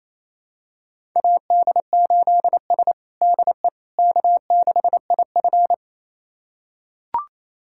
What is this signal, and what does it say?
Morse code, heard as a keyed tone, about 28 wpm, 700 Hz. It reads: AD8S DE K6IF
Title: USA Callsign DE Callsign - 4 Characters - 28wpm